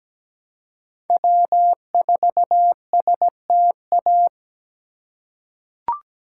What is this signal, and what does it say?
Morse code, heard as a keyed tone, about 17 wpm, 700 Hz. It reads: W4STA
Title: USA Call Signs - 5 Characters - 4x Character Spacing - 17wpm